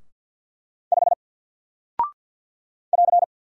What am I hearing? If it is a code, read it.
Morse code, heard as a keyed tone, about 50 wpm, 700 Hz. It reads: 5 <AR>
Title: Single Character (Letter:Number:Punctuation:Prosign) 50wpm